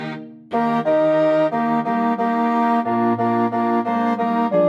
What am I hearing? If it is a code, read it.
There is no code.